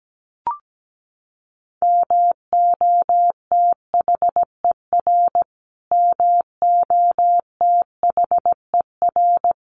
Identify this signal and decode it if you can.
Morse code, heard as a keyed tone, about 17 wpm, 700 Hz. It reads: MOTHER MOTHER
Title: Top 500 Words Repeated 17wpm